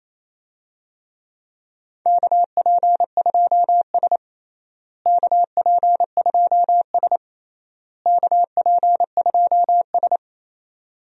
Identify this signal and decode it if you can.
Morse code, heard as a keyed tone, about 28 wpm, 700 Hz. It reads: KP2S KP2S KP2S
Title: DXCC-100 Call Signs - 4 Characters - Repeated 3x - 28wpm